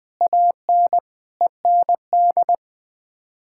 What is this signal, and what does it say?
Morse code, heard as a keyed tone, about 20 wpm, 700 Hz. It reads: AN END
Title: Sets of 8 Words - 20wpm